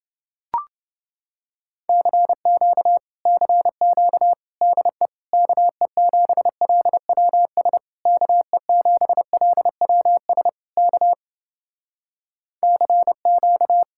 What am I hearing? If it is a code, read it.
Morse code, heard as a keyed tone, about 30 wpm, 700 Hz. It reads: CQ CQ DE KE7LWS KE7LWS K CQ